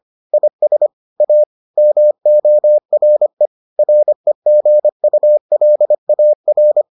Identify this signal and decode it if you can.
Morse code, heard as a keyed tone, about 25 wpm, 600 Hz. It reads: IS A MORE REGULAR